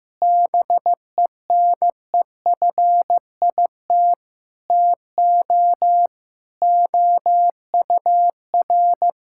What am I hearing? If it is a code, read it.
Morse code, heard as a keyed tone, about 16 wpm, 700 Hz. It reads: BENEFIT TO OUR
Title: Sentences from Top 1000 Words - Review Part 6 - 15wpm